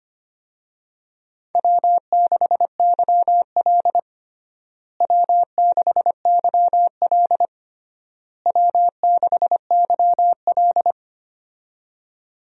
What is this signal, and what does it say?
Morse code, heard as a keyed tone, about 25 wpm, 700 Hz. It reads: W6YL W6YL W6YL
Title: USA Call Signs - 4 Characters - Repeated 3x - 25wpm